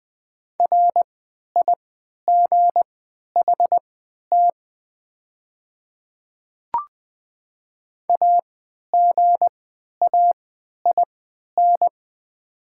Morse code, heard as a keyed tone, about 20 wpm, spaced as for 11 wpm, 700 Hz. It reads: RIGHT AGAIN
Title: 3x Character Spacing - Top 200 Words - 20wpm